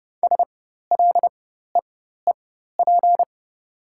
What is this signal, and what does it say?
Morse code, heard as a keyed tone, about 29 wpm, spaced as for 14 wpm, 690 Hz. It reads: SLEEP
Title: 4x Character Spacing - Top 500 Words - 30wpm